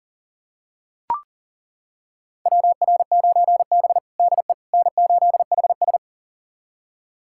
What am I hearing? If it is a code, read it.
Morse code, heard as a keyed tone, about 40 wpm, 700 Hz. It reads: WR9B DE N8HS
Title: USA Callsign DE Callsign - 4 Characters - 40wpm